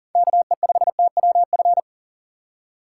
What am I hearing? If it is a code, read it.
Morse code, heard as a keyed tone, about 40 wpm, 700 Hz. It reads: KE5TWF
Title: USA Call Signs - 6 Characters - 2x Character Spacing - 40wpm